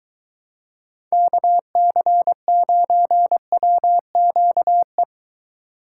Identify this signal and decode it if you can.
Morse code, heard as a keyed tone, about 22 wpm, 700 Hz. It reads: KC9WQE